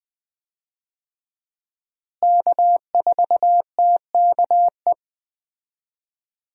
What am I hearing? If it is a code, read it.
Morse code, heard as a keyed tone, about 21 wpm, 700 Hz. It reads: K4TKE
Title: USA Call Signs - 5 Characters - Repeated 3x - 20wpm